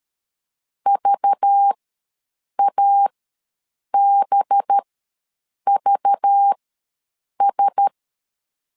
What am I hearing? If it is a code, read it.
Morse code, heard as a keyed tone, about 13 wpm, spaced as for 7 wpm, 800 Hz. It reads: VABVS
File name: LEZIONI / 5 LBVC / 5 LBV ws 12wpm 3015